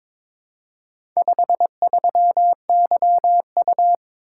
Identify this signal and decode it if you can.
Morse code, heard as a keyed tone, about 22 wpm, 700 Hz. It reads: 53YU
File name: practice / 22wpm / Four Letter-Number Combinations - 22wpm